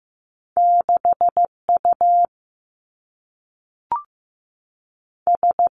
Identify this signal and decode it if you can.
Morse code, heard as a keyed tone, about 15 wpm, 700 Hz. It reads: 6 U S
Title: Two Challenging Letter-Number Combinations 15wpm